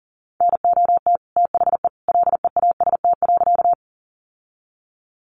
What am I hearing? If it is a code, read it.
Morse code, heard as a keyed tone, about 40 wpm, 700 Hz. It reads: NOT THE LEAST.
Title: Sentences from Top 700 Words 40wpm